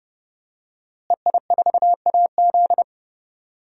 Morse code, heard as a keyed tone, about 30 wpm, 700 Hz. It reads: EI4AZ
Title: DXCC-100 Call Signs - 5 Characters - Repeated 2x - 30wpm